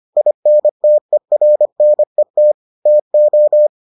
Morse code, heard as a keyed tone, about 25 wpm, 600 Hz. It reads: INTERNET TO